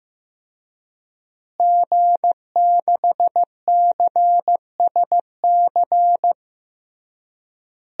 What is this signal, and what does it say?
Morse code, heard as a keyed tone, about 16 wpm, 700 Hz. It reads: G6CSC
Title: DXCC-100 Call Signs - 5 Characters - Repeated 3x - 15wpm